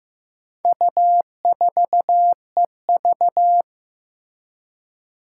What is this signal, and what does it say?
Morse code, heard as a keyed tone, about 15 wpm, 700 Hz. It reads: U4EV
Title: Four Challenging Letter-Number Combinations 15wpm